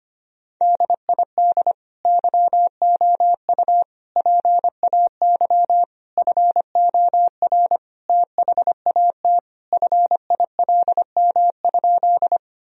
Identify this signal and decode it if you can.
Morse code, heard as a keyed tone, about 25 wpm, 700 Hz. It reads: DID YOU PAY FOR THAT FILM?
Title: Sentences from Top 1000 Words - Review Part 3 - 25wpm